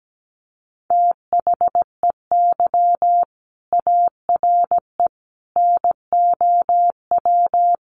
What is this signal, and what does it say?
Morse code, heard as a keyed tone, about 17 wpm, 700 Hz. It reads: THEY ARE NOW
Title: Sentences from Top 900 Words 17wpm